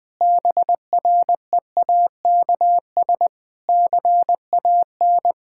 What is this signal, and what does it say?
Morse code, heard as a keyed tone, about 20 wpm, 700 Hz. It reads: BREAKS CAN
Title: Difficult Sentences with Hints - 20wpm